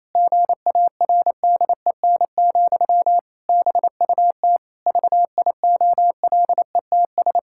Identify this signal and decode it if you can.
Morse code, heard as a keyed tone, about 28 wpm, 700 Hz. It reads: GARDEN, BUT VIOLETS
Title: Pangram Challenge v2 - 28wpm